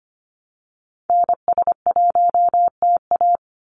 Morse code, heard as a keyed tone, about 25 wpm, 700 Hz. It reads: NS1TA
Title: NA SOTA Activator Callsigns 25wpm